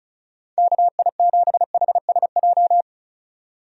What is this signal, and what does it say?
Morse code, heard as a keyed tone, about 34 wpm, 700 Hz. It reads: KI7HSJ
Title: USA Call Signs - 6 Characters - 35wpm